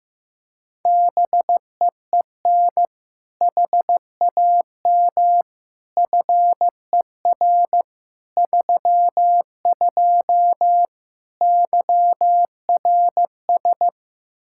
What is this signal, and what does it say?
Morse code, heard as a keyed tone, about 15 wpm, 700 Hz. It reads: BEEN HAM FER 32 YRS
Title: K1USN SST QSO Practice 15wpm